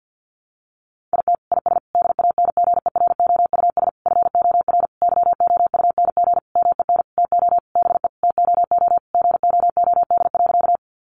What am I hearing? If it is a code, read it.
Morse code, heard as a keyed tone, about 50 wpm, 700 Hz. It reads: IT IS DANGEROUS FOR YOUNG MEN TO BE TOO GOOD.
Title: Sentences from Top 1200 Words 50wpm